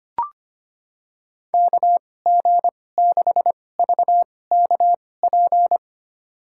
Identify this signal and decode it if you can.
Morse code, heard as a keyed tone, about 25 wpm, spaced as for 18 wpm, 700 Hz. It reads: KG6VKP
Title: USA Call Signs - 6 Characters - 2x Character Spacing - 25wpm